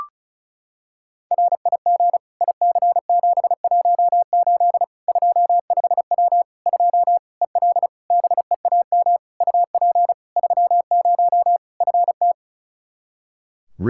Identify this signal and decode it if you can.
Morse code, heard as a keyed tone, about 34 wpm, 700 Hz. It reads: RIG IC718 25W 2 EL BEAM UP 30 FT
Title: Practice QSO - Exchange 2 - Info - 35wpm